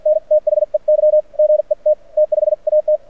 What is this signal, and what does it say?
Morse code, heard as a keyed tone, about 52 wpm, spaced as for 44 wpm, 590 Hz. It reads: N THEY GET T5AT
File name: archived / CRSnet